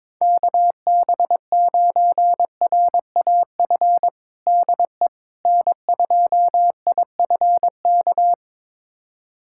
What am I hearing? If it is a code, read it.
Morse code, heard as a keyed tone, about 22 wpm, 700 Hz. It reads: KB9RAF DE N2IFK
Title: USA Callsign DE Callsign 22wpm